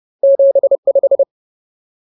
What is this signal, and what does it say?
Morse code, heard as a keyed tone, about 30 wpm, 550 Hz. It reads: 75
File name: Ultimate Word List / 30WPM Ultimate